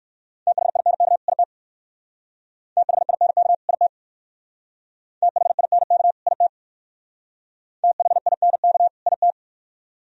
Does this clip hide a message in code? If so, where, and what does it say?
Yes: Morse code from 0.5 to 9.3 s, heard as a keyed tone, about 64 wpm, 700 Hz. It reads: THINK IT THINK IT THINK IT THINK IT